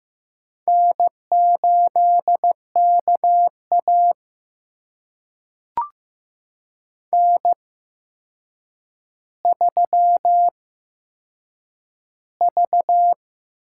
Morse code, heard as a keyed tone, about 15 wpm, spaced as for 7 wpm, 700 Hz. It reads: N8KA N3V
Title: USA Call Signs - 4 Characters - 8x Character Spacing - 15wpm